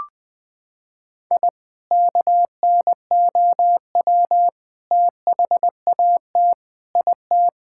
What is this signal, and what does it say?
Morse code, heard as a keyed tone, about 21 wpm, 700 Hz. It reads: I KNOW THAT IT
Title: Sentences from Top 1000 Words - Review Part 6 - 20wpm